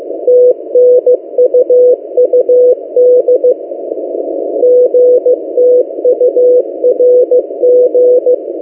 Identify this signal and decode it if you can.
Morse code, heard as a keyed tone, about 15 wpm, 500 Hz. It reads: TNUUD GTURG
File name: M08a- -1355utc-8096khz